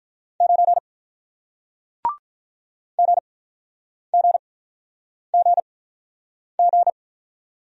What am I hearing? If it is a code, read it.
Morse code, heard as a keyed tone, about 47 wpm, spaced as for 9 wpm, 700 Hz. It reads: 9 GGGG